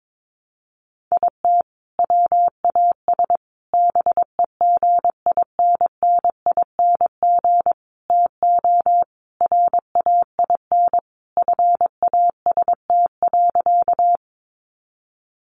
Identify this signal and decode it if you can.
Morse code, heard as a keyed tone, about 22 wpm, 700 Hz. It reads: IT WAS BEGINNING TO RAIN FAST.
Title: Sentences from Top 800 Words 22wpm